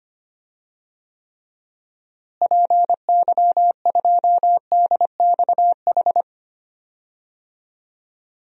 Morse code, heard as a keyed tone, about 25 wpm, 700 Hz. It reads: PY2DXH
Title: DXCC-100 Call Signs - 6 Characters - Repeated 3x - 25wpm